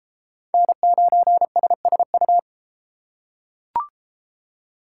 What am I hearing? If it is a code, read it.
Morse code, heard as a keyed tone, about 31 wpm, 700 Hz. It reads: N9SSU